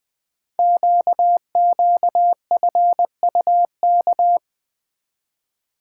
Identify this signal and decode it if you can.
Morse code, heard as a keyed tone, about 20 wpm, 700 Hz. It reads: QQFUK
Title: Code Groups - 5 Letters - 20wpm